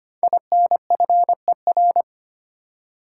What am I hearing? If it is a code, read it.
Morse code, heard as a keyed tone, about 25 wpm, 700 Hz. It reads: INFER
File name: 5 Letter Words - First 3 Letters Given - 25wpm